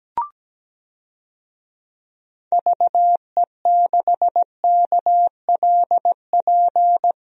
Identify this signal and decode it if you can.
Morse code, heard as a keyed tone, about 17 wpm, 700 Hz. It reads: VE6KLP
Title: DXCC-100 Call Signs - 6 Characters - Repeated 2x - 17wpm